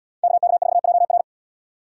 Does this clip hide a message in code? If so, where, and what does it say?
Yes: Morse code from 0.2 to 1.2 s, heard as a keyed tone, about 70 wpm, 700 Hz. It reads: KK4JR